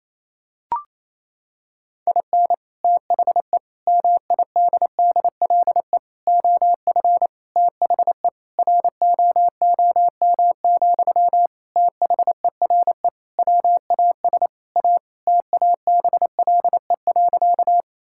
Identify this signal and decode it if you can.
Morse code, heard as a keyed tone, about 28 wpm, 700 Hz. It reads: IN THE MIDDLE OF THE ROOM, THERE WAS A TABLE.
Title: Sentences from Top 1000 Words - Review Part 6 - 28wpm